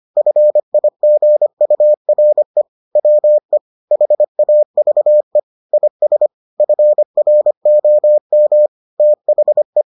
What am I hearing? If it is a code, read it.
Morse code, heard as a keyed tone, about 25 wpm, 600 Hz. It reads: FIGURE WE HAVE IS FROM THE